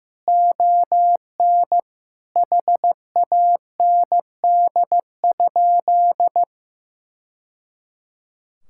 Morse code, heard as a keyed tone, about 14 wpm, 700 Hz. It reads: ON HAND?
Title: Sentences from Top 1000 Words - Review Part 5 - 15wpm